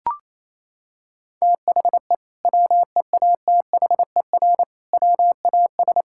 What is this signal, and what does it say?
Morse code, heard as a keyed tone, about 28 wpm, 700 Hz. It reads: THE WEATHER WAS
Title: Sentences from Top 1000 Words - Review Part 5 - 28wpm